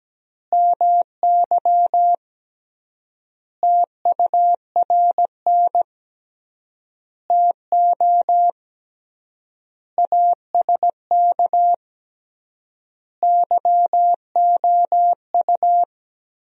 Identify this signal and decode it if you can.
Morse code, heard as a keyed tone, about 17 wpm, 700 Hz. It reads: MY TURN TO ASK YOU
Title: 3x Word-Spacing - Sentences from Top 200 Words - 17wpm